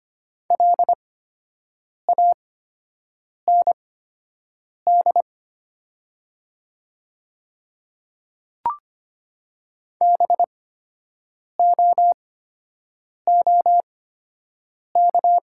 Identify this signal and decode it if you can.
Morse code, heard as a keyed tone, about 25 wpm, spaced as for 7 wpm, 700 Hz. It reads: LAND BOOK
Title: 8x Character Spacing - Top 500 Words - 25wpm